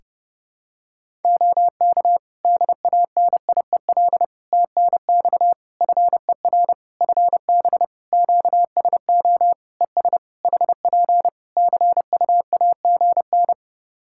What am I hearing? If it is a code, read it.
Morse code, heard as a keyed tone, about 30 wpm, 700 Hz. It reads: OK DANIEL TNX FER FB QSO ES HP CUAGN
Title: Practice QSO - Exchange 4 - Ending - 30wpm